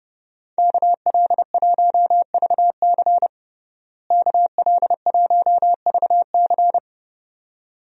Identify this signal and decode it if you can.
Morse code, heard as a keyed tone, about 30 wpm, 700 Hz. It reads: KL1VC KL1VC
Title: USA Call Signs - 5 Characters - Repeated 2x - 30wpm